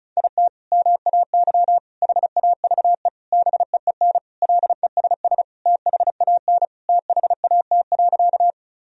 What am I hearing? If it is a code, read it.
Morse code, heard as a keyed tone, about 34 wpm, 700 Hz. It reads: IT MAY HAVE BEEN LESS THAN THAT.
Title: Sentences from Top 1000 Words - Review Part 2 - 35wpm